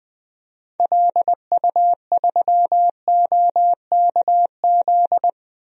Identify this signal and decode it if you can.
Morse code, heard as a keyed tone, about 20 wpm, 700 Hz. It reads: LU3OKZ
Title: DXCC-100 Call Signs - 6 Characters - 20wpm